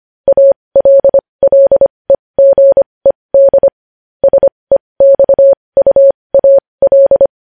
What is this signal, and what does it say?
Morse code, heard as a keyed tone, about 25 wpm, spaced as for 20 wpm, 550 Hz. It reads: ALLEGED SEXUAL
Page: news-headlines-in-morse-code-at-20-wpm